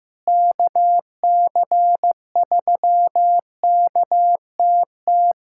Morse code, heard as a keyed tone, about 15 wpm, 700 Hz. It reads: KC3KTT
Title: USA Callsign DE Callsign 15wpm